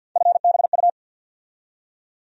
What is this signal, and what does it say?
Morse code, heard as a keyed tone, about 48 wpm, 700 Hz. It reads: FBU
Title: USA Call Sign Postfixes - 3 Letters 50wpm